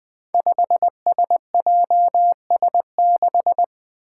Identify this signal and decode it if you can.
Morse code, heard as a keyed tone, about 20 wpm, 700 Hz. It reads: 5SJS6
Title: Five Challenging Letter-Number Combinations 20wpm